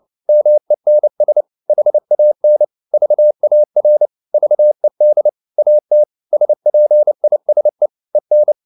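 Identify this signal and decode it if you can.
Morse code, heard as a keyed tone, about 29 wpm, 600 Hz. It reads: MENS HAN VAR VED AT SPISE EN
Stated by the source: Fast (QRQ) Morse at 29 wpm, from CW med Gnister.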